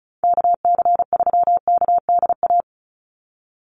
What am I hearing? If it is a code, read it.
Morse code, heard as a keyed tone, about 34 wpm, 700 Hz. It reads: KC3KDA
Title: USA Call Signs 35wpm